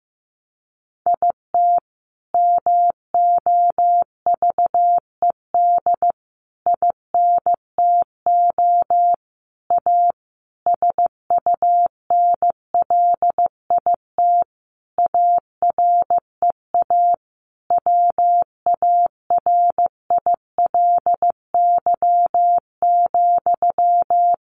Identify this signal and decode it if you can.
Morse code, heard as a keyed tone, about 15 wpm, 700 Hz. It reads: IT MOVED INTO A SUNLIT AREA WARILY,
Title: A Sound Of Thunder 15wpm